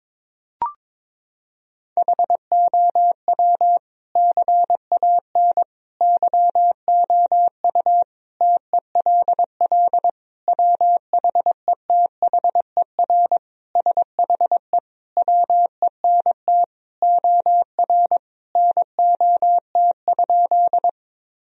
Morse code, heard as a keyed tone, about 22 wpm, 700 Hz. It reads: HOW CAN YOU TELL WHETHER SHE WENT OR NOT?
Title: Sentences from Top 1000 Words - Review Part 3 - 22wpm